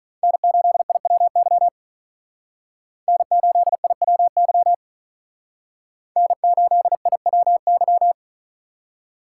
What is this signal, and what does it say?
Morse code, heard as a keyed tone, about 44 wpm, 700 Hz. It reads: N8IWY N8IWY N8IWY